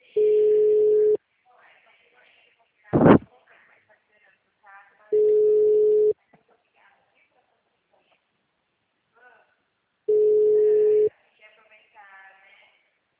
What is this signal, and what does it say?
Ringback tone (Europe, 425 Hz), with speech.
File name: out- -2039- - - .29910